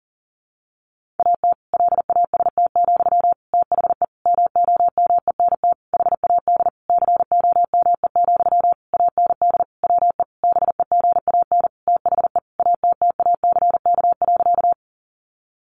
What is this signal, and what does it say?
Morse code, heard as a keyed tone, about 40 wpm, 700 Hz. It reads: AT LAST, THE MOMENT HAD COME, AND WE BEGAN THE ATTACK.
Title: Sentences from Top 1500 Words 40wpm